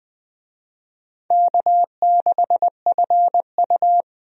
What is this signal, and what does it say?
Morse code, heard as a keyed tone, about 20 wpm, 700 Hz. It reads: K6FU
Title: USA Call Signs - 4 Characters - Repeated 3x - 20wpm